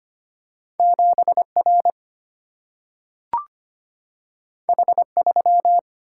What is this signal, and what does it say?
Morse code, heard as a keyed tone, about 25 wpm, 700 Hz. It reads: 7 R H 3
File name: Two Letter-Number Combinations - 25wpm